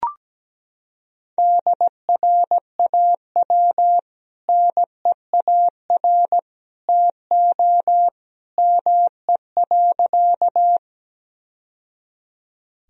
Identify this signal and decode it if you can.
Morse code, heard as a keyed tone, about 17 wpm, 700 Hz. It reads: DRAW NEAR TO ME.
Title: Sentences from Top 1000 Words - Review Part 1 - 17wpm